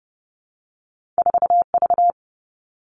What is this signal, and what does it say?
Morse code, heard as a keyed tone, about 30 wpm, 700 Hz. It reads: 4 V
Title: Two Challenging Letter-Number Combinations 30wpm